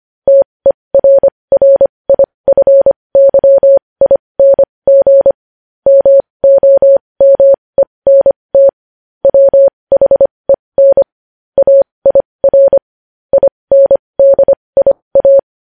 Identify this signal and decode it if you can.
Morse code, heard as a keyed tone, about 25 wpm, spaced as for 20 wpm, 550 Hz. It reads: TERRIFYING MOMENT WHEN AIR INDIA
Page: news headlines in morse code at 20 wpm